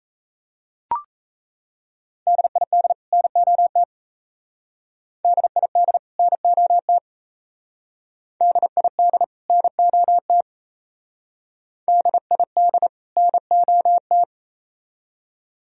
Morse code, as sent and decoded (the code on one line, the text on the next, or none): -.. .. -.. / -. --- - / -.. .. -.. / -. --- - / -.. .. -.. / -. --- - / -.. .. -.. / -. --- -
DID NOT DID NOT DID NOT DID NOT